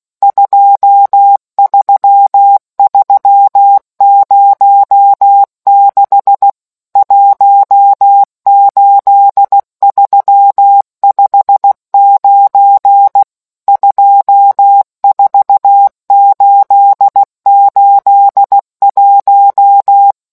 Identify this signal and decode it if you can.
Morse code, heard as a keyed tone, about 16 wpm, 800 Hz. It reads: 23306 18359 24881